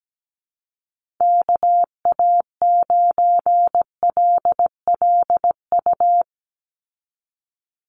Morse code, heard as a keyed tone, about 17 wpm, 700 Hz. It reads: KA9LLU
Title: USA Call Signs 17wpm